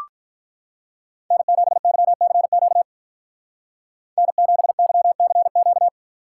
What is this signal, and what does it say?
Morse code, heard as a keyed tone, about 43 wpm, 700 Hz. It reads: N7YCQ N7YCQ
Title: Speed Racing - USA 1x3 Call Signs 35wpm